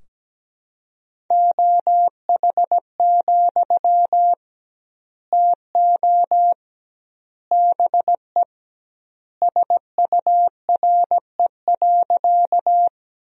Morse code, heard as a keyed tone, about 17 wpm, 700 Hz. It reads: OH, TO BE SURE.
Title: 2x Word-Spacing - Sentences from Top 500 Words - 17wpm